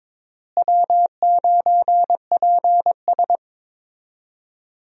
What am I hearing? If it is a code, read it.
Morse code, heard as a keyed tone, about 21 wpm, 700 Hz. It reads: W9PS
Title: USA Call Signs - 4 Characters - Repeated 3x - 22wpm